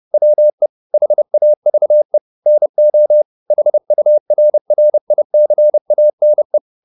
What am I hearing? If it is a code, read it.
Morse code, heard as a keyed tone, about 30 wpm, 600 Hz. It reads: WE HAVE NO HURRICANE
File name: the old man and the sea / the old man and the sea 1 30wpm